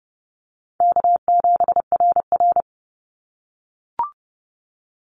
Morse code, heard as a keyed tone, about 30 wpm, 700 Hz. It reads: K7RR
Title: Easy Call Signs - 30wpm